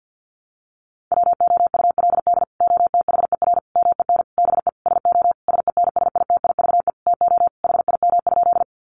Morse code, heard as a keyed tone, about 50 wpm, 700 Hz. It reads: WOULD OTHER MEN BE SO SENSITIVE TO HIM?
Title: Sentences from Top 1400 Words 50wpm